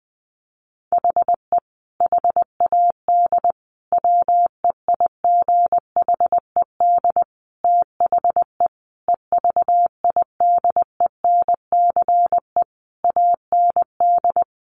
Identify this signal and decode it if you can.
Morse code, heard as a keyed tone, about 20 wpm, 700 Hz. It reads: HE HAD WEIGHED THE EVIDENCE AND
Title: A Sound Of Thunder 20wpm